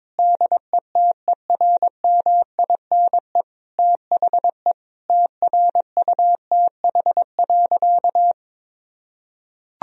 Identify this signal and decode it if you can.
Morse code, heard as a keyed tone, about 22 wpm, 700 Hz. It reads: DETERMINE THE TRUTH.
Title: Sentences from Top 1000 Words - Review Part 4 - 22wpm